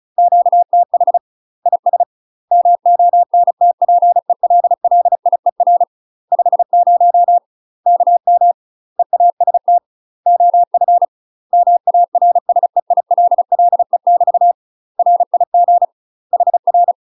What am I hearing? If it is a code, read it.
Morse code, heard as a keyed tone, about 35 wpm, 700 Hz. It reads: QTH IS MONTPELLIER 50 KM EAST OF MARSEILLE <BT> RIG HR
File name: UK MASTER QSOs TRAINING / 35WPM / UK MASTER QSOs 0009